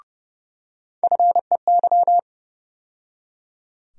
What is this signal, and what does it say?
Morse code, heard as a keyed tone, about 31 wpm, 700 Hz. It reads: FEY
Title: USA Call Sign Postfixes - 3 Letters 30wpm